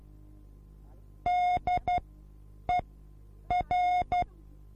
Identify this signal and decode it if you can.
Morse code, heard as a keyed tone, about 12 wpm, spaced as for 8 wpm, 720 Hz. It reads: DER
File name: repos / Morsekurs / DJ4UF Morsen24